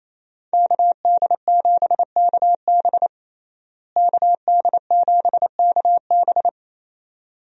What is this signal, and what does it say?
Morse code, heard as a keyed tone, about 28 wpm, 700 Hz. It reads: KD7KB KD7KB
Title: USA Call Signs - 5 Characters - Repeated 2x - 28wpm